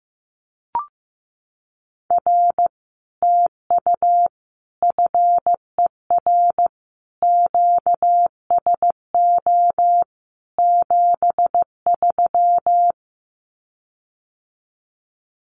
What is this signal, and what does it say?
Morse code, heard as a keyed tone, about 15 wpm, 700 Hz. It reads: R TU FER QSO 73
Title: Common QSO Phrases Encore 15wpm